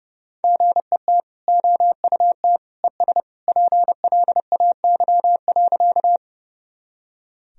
Morse code, heard as a keyed tone, about 32 wpm, 700 Hz. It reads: GET OUT ES PLAY.
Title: Sentences from Top 200 Words and Abbreviations 30wpm